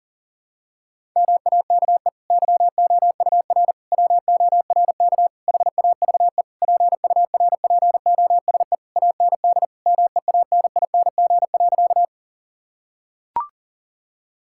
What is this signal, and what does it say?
Morse code, heard as a keyed tone, about 40 wpm, 700 Hz. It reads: MAKE YOUR WORK HAVE PURPOSE AND MEANING.
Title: Sentences from Top 1000 Words - Review Part 6 - 40wpm